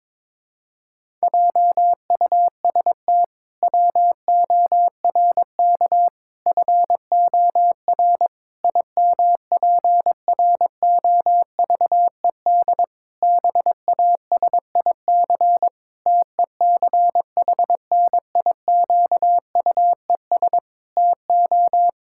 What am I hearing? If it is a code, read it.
Morse code, heard as a keyed tone, about 22 wpm, 700 Hz. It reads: JUST WORK FOR IMPROVED BASIC TECHNIQUES TO
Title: Pangram Challenge v2 - 22wpm